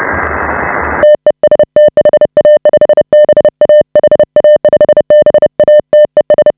Morse code, heard as a keyed tone, about 30 wpm, 590 Hz. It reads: TEST HA5BA HA5BA TES
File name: AUDIO2019 / HA5BA CW